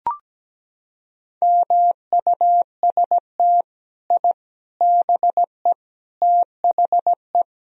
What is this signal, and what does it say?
Morse code, heard as a keyed tone, about 17 wpm, 700 Hz. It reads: MUST I BE THE
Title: Sentences from Top 1000 Words - Review Part 4 - 17wpm